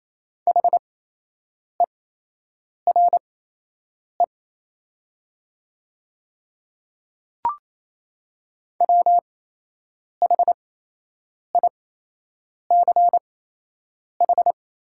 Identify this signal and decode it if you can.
Morse code, heard as a keyed tone, about 28 wpm, spaced as for 8 wpm, 700 Hz. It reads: HERE WHICH
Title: 8x Character Spacing - Top 200 Words - 28wpm